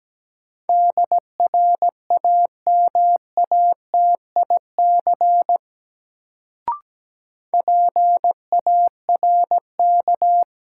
Morse code, heard as a keyed tone, about 17 wpm, 700 Hz. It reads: DRAMATIC PARK
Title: Sentences from Top 1300 Words - New Word Drill Alt - 17wpm